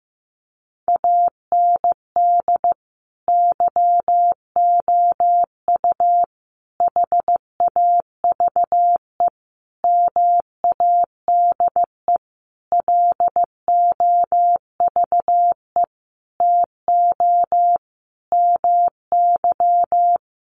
Morse code, heard as a keyed tone, about 15 wpm, 700 Hz. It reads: AND YOU HAVE MADE LOVE TO MY
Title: Sentences from Top 1500 Words 15wpm